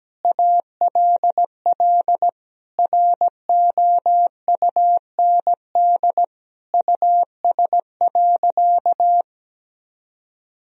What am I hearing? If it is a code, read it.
Morse code, heard as a keyed tone, about 17 wpm, 700 Hz. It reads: ALL ROUND US.
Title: Sentences from Top 1000 Words - Review Part 1 - 17wpm